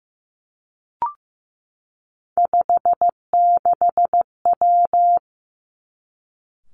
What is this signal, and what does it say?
Morse code, heard as a keyed tone, about 15 wpm, 700 Hz. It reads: 56W
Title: Three Challenging Letter-Number Combinations 15wpm